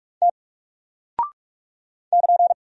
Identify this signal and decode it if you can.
Morse code, heard as a keyed tone, about 44 wpm, 700 Hz. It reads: T <KN>
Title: Single Character (Letter:Number:Punctuation:Prosign) 45wpm